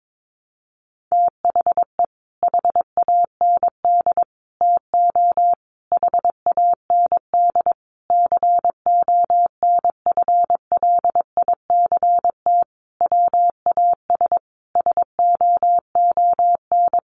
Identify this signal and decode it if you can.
Morse code, heard as a keyed tone, about 22 wpm, 700 Hz. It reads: THE HAND TO HAND CONFLICT WAS SOON
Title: Sentences from Top 1300 Words 22wpm